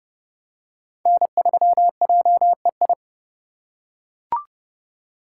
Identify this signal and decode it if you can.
Morse code, heard as a keyed tone, about 29 wpm, 700 Hz. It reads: N3JEI